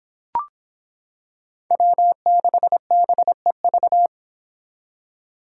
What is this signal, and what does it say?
Morse code, heard as a keyed tone, about 25 wpm, 700 Hz. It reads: W6BEV